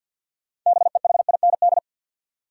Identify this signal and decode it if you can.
Morse code, heard as a keyed tone, about 50 wpm, 700 Hz. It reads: BEHIND
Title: Sentences from Top 500 Words - New Word Drill - 50wpm